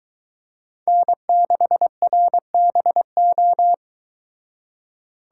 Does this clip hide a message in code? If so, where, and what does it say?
Yes: Morse code from 0.9 to 3.7 s, heard as a keyed tone, about 23 wpm, 700 Hz. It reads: N6RBO